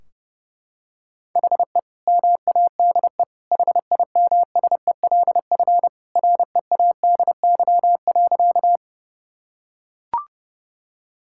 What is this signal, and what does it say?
Morse code, heard as a keyed tone, about 32 wpm, 700 Hz. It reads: HE MADE HIMSELF READY.
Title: Sentences from Top 1600 Words 30wpm